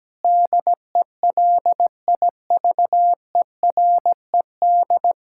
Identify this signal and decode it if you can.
Morse code, heard as a keyed tone, about 17 wpm, 700 Hz. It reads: DELIVERED
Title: Pangram Challenge v2 - 17wpm